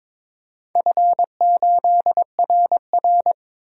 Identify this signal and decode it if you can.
Morse code, heard as a keyed tone, about 23 wpm, 700 Hz. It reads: F8RR
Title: DXCC-100 Call Signs - 4 Characters - 22wpm